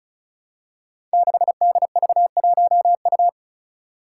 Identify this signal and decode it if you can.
Morse code, heard as a keyed tone, about 36 wpm, 700 Hz. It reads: 6DV1U
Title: Five Challenging Letter-Number Combinations 35wpm